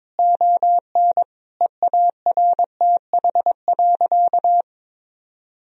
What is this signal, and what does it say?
Morse code, heard as a keyed tone, about 22 wpm, 700 Hz. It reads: ON EARTH.
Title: Sentences from Top 1000 Words - Review Part 2 - 22wpm